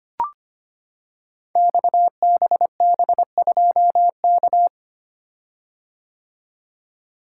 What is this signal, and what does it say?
Morse code, heard as a keyed tone, about 25 wpm, 700 Hz. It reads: XBB2K
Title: Code Groups - 5 Letter:Numbers - 25wpm